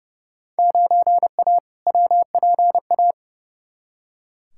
Morse code, heard as a keyed tone, about 31 wpm, 700 Hz. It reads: 9A WPA
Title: Field Day Exchanges Encore - 30wpm